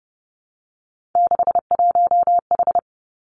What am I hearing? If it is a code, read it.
Morse code, heard as a keyed tone, about 32 wpm, 700 Hz. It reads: 61H
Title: Three Challenging Letter-Number Combinations 30wpm